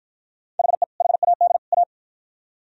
Morse code, heard as a keyed tone, about 51 wpm, 700 Hz. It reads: HE HAD A